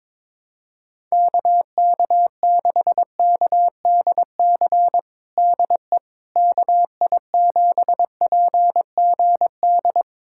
Morse code, heard as a keyed tone, about 22 wpm, 700 Hz. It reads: KK6KDC DE KI7PGD
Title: USA Callsign DE Callsign 22wpm